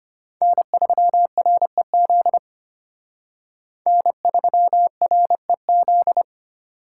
Morse code, heard as a keyed tone, about 33 wpm, 700 Hz. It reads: N3REZ N3REZ